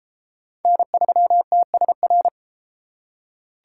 Morse code, heard as a keyed tone, about 29 wpm, 700 Hz. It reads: N3TSR